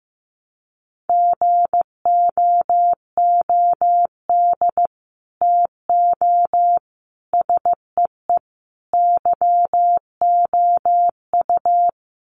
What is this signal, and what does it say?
Morse code, heard as a keyed tone, about 15 wpm, 700 Hz. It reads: GOOD TO SEE YOU
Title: Sets of 4 Words 15wpm